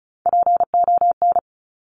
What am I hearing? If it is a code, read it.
Morse code, heard as a keyed tone, about 36 wpm, 700 Hz. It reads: PON
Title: Top Three Letter Combinations in Words 35wpm